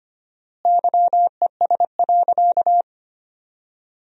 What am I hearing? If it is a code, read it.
Morse code, heard as a keyed tone, about 25 wpm, 700 Hz. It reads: YES.
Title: 4x Word-Spacing - Sentences from Top 500 Words - 25wpm